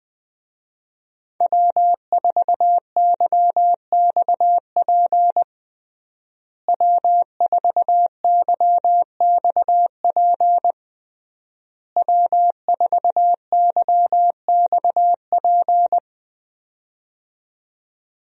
Morse code, heard as a keyed tone, about 20 wpm, 700 Hz. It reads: W4YXP W4YXP W4YXP
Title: USA Call Signs - 5 Characters - Repeated 3x - 20wpm